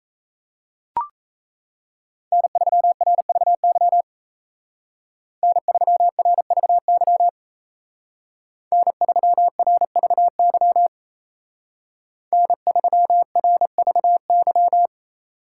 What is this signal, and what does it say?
Morse code, heard as a keyed tone, about 33 wpm, 700 Hz. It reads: N3RVY N3RVY N3RVY N3RVY
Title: Speed Racing - USA Call Signs 28wpm